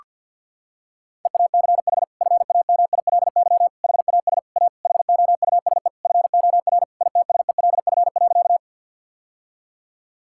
Morse code, heard as a keyed tone, about 50 wpm, 700 Hz. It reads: EACH FAMILY HAS A HOUSE FOR ITSELF.
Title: Sentences from Top 1000 Words - Review Part 5 - 50wpm